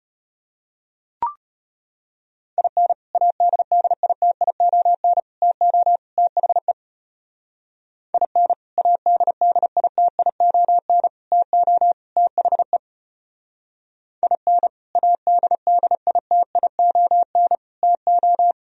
Morse code, heard as a keyed tone, about 30 wpm, 700 Hz. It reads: IN ADDITION TO THE IN ADDITION TO THE IN ADDITION TO
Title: Speed-Racing - Sets of 4 Words 25wpm